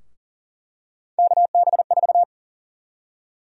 Morse code, heard as a keyed tone, about 41 wpm, 700 Hz. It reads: KB4
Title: USA Call Sign Prefixes - Repeated 2x - 40wpm